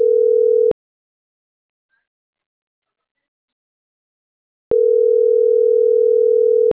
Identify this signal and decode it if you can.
Ringback tone (North America), 440 + 480 Hz.